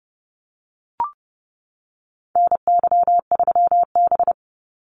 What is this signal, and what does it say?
Morse code, heard as a keyed tone, about 30 wpm, 700 Hz. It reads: NY3B
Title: Call Signs - 30wpm